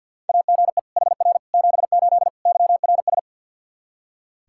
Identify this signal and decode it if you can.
Morse code, heard as a keyed tone, about 47 wpm, 700 Hz. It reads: AGE HR 78 YRS
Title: Practice QSO - Exchange 3 - More Info - 50wpm